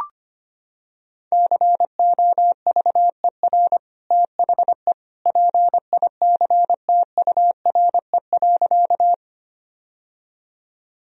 Morse code, heard as a keyed tone, about 25 wpm, 700 Hz. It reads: COVER THE PICTURE.
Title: Sentences from Top 1000 Words - Review Part 1 - 25wpm